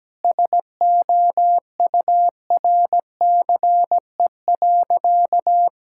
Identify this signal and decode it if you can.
Morse code, heard as a keyed tone, about 17 wpm, 700 Hz. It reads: SOURCE.
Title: Sentences from Top 1000 Words - Review Part 5 - 17wpm